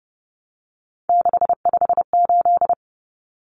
Morse code, heard as a keyed tone, about 30 wpm, 700 Hz. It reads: 658
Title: Numbers - 3-Digit 30wpm